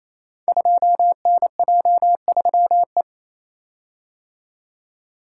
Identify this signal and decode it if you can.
Morse code, heard as a keyed tone, about 27 wpm, 690 Hz. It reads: 2NJ3E
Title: Code Groups - 5 Letter:Numbers - 28wpm